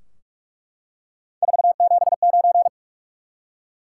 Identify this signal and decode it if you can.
Morse code, heard as a keyed tone, about 47 wpm, 700 Hz. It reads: 479
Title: Top 180 QSO Elements 45wpm